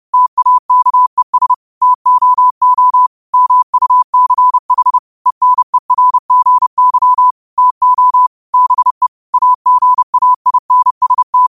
Morse code, heard as a keyed tone, about 30 wpm, 1000 Hz. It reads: TAKES TOO MUCH ENERGY TO BE AGAINST